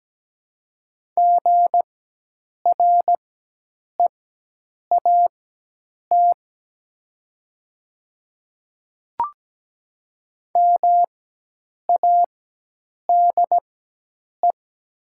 Morse code, heard as a keyed tone, about 17 wpm, spaced as for 8 wpm, 700 Hz. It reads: GREAT MADE
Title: 4x Character Spacing - Top 200 Words - 17wpm